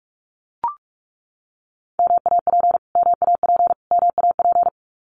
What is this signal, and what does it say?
Morse code, heard as a keyed tone, about 44 wpm, 700 Hz. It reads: MAP MAP MAP
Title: Top 500 Words Repeated 45wpm